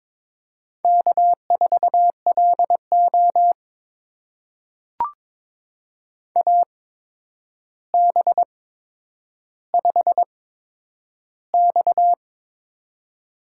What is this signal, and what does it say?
Morse code, heard as a keyed tone, about 22 wpm, spaced as for 9 wpm, 700 Hz. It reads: K4LO AB5X
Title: USA Call Signs - 8x Character Spacing - 22wpm